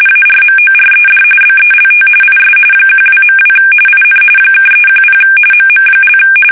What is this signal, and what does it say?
RTTY, about 45.45 baud. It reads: ARMORED BELT IS 12.2 IN (310 MM)
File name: fsk850 / USS-Massachusetts